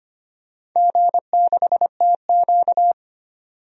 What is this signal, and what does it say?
Morse code, heard as a keyed tone, about 25 wpm, 700 Hz. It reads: G6TQ
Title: DXCC-100 Call Signs - 4 Characters - 25wpm